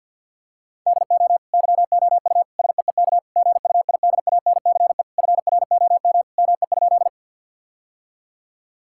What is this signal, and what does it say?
Morse code, heard as a keyed tone, about 50 wpm, 700 Hz. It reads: DO YOU SEEK GUIDANCE FROM ME?
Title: Sentences from Top 1600 Words 50wpm